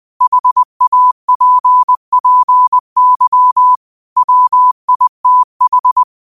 Morse code, heard as a keyed tone, about 20 wpm, 1000 Hz. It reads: HAPPY WITH
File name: qotdfiles / 20wpm